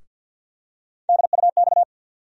Morse code, heard as a keyed tone, about 52 wpm, 700 Hz. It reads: DUX
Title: USA Call Sign Postfixes - 3 Letters 50wpm